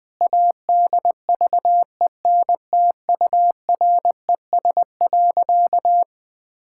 Morse code, heard as a keyed tone, about 19 wpm, 700 Hz. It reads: ADVENTURES.
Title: Sentences with 2-Word Phrases - Spoken Beginning - 20wpm